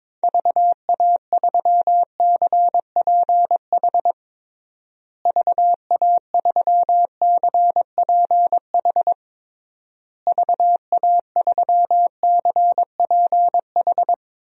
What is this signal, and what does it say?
Morse code, heard as a keyed tone, about 22 wpm, 700 Hz. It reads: VA3CPH VA3CPH VA3CPH
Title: DXCC-100 Call Signs - 6 Characters - Repeated 3x - 22wpm